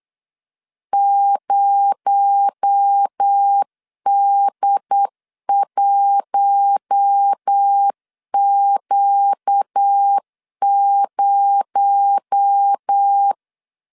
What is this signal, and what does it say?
Morse code, heard as a keyed tone, about 8 wpm, 800 Hz. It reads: 0D1Q0
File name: LEZIONI / 6 PKX0 / 6 PKX0 8wpm 3341